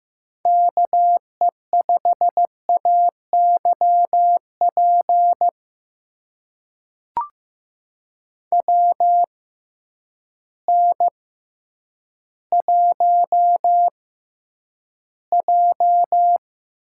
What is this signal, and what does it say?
Morse code, heard as a keyed tone, about 15 wpm, spaced as for 9 wpm, 700 Hz. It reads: KE5AYP WN1J
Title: USA Call Signs - 6x Character Spacing - 15wpm